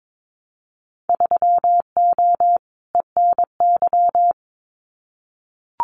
Morse code, heard as a keyed tone, about 22 wpm, 700 Hz. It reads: 3O ENY
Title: Winter Field Day Exchanges Encore 22wpm